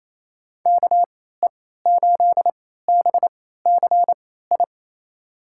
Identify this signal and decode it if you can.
Morse code, heard as a keyed tone, about 28 wpm, spaced as for 16 wpm, 700 Hz. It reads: KE8BCI
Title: USA Call Signs - 6 Characters - 3x Character Spacing - 28wpm